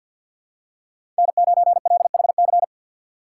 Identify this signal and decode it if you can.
Morse code, heard as a keyed tone, about 48 wpm, 700 Hz. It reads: N9LHC
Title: USA Call Signs - 2x Character Spacing - 50wpm